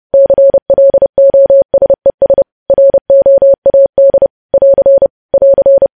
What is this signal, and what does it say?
Morse code, heard as a keyed tone, about 30 wpm, 550 Hz. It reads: CLOSES ROAD <AR> <AR>
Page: news headlines in morse code at 30 wpm